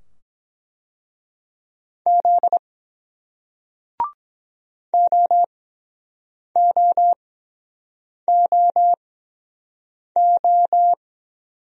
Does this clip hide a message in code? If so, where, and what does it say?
Yes: Morse code from 2.1 to 10.9 s, heard as a keyed tone, about 23 wpm, spaced as for 7 wpm, 700 Hz. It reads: Z OOOO